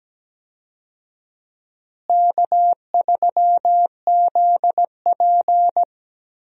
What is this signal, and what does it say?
Morse code, heard as a keyed tone, about 17 wpm, 700 Hz. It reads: K3ZP
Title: USA Call Signs - 4 Characters - Repeated 2x - 17wpm